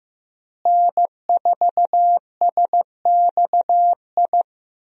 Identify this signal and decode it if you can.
Morse code, heard as a keyed tone, about 15 wpm, 700 Hz. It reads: N4SXI
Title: USA Call Signs - 5 Characters - Repeated 3x - 15wpm